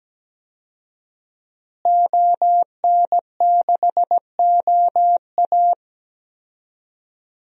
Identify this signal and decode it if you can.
Morse code, heard as a keyed tone, about 17 wpm, 700 Hz. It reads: ON6OA
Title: DXCC-100 Call Signs - 5 Characters - Repeated 3x - 17wpm